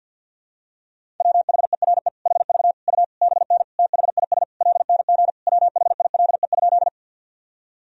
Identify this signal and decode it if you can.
Morse code, heard as a keyed tone, about 52 wpm, 700 Hz. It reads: WHERE HV U BN THIS LNG WHILE?
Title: Sentences from Top 300 Words and Abbreviations 50wpm